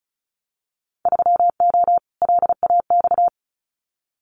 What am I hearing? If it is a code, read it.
Morse code, heard as a keyed tone, about 36 wpm, 700 Hz. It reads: 3O LAX
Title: Winter Field Day Exchanges Encore 35wpm